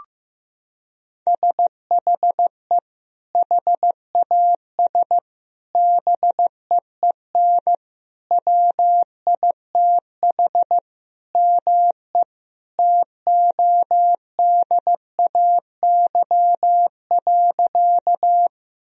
Morse code, heard as a keyed tone, about 15 wpm, 700 Hz. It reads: SHE HAS BEEN WITH ME TODAY.
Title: Sentences from Top 1000 Words - Review Part 4 - 15wpm